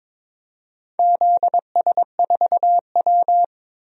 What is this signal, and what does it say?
Morse code, heard as a keyed tone, about 22 wpm, 700 Hz. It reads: ZS4W
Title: DXCC-100 Call Signs - 4 Characters - Repeated 3x - 22wpm